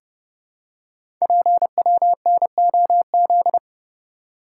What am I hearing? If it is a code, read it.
Morse code, heard as a keyed tone, about 30 wpm, 690 Hz. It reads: PWNOZ
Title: Code Groups - 5 Letters - 30wpm